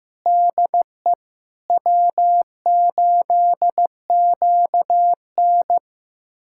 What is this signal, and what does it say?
Morse code, heard as a keyed tone, about 15 wpm, 700 Hz. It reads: DE W8QN
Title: USA Callsign DE Callsign - 4 Characters - 15wpm